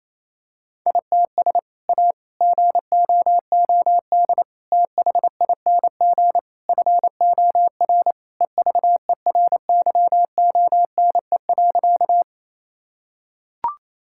Morse code, heard as a keyed tone, about 28 wpm, 700 Hz. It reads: ITS A GOOD THING FOR EVERYONE.
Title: Sentences from Top 1000 Words - Review Part 5 - 28wpm